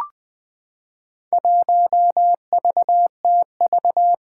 Morse code, heard as a keyed tone, about 20 wpm, 700 Hz. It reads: 1VTV
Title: Four Challenging Letter-Number Combinations 20wpm